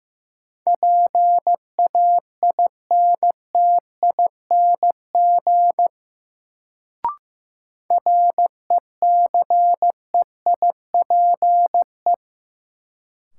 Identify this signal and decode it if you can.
Morse code, heard as a keyed tone, about 15 wpm, 700 Hz. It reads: PAINTING RECEIPE
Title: Sentences from Top 1200 Words - New Word Drill Alt - 15wpm